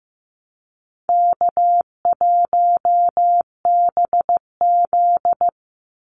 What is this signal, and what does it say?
Morse code, heard as a keyed tone, about 15 wpm, 700 Hz. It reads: K1BZ
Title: Easy Call Signs - 15wpm